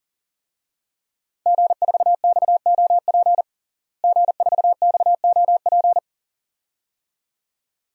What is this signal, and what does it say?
Morse code, heard as a keyed tone, about 40 wpm, 700 Hz. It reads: G4XOP G4XOP
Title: DXCC-100 Call Signs - 5 Characters - Repeated 2x - 40wpm